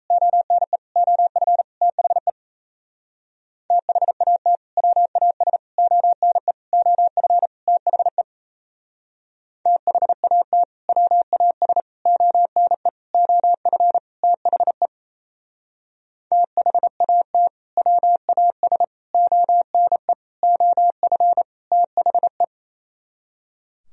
Morse code, heard as a keyed tone, about 32 wpm, 700 Hz. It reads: ONE OF THE THAT WAS ONE OF THE THAT WAS ONE OF THE THAT WAS ONE OF THE
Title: Speed-Racing - Sets of 5 Words 28wpm